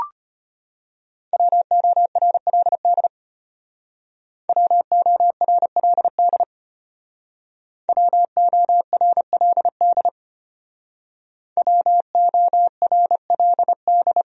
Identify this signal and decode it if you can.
Morse code, heard as a keyed tone, about 33 wpm, 700 Hz. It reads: WORLD WORLD WORLD WORLD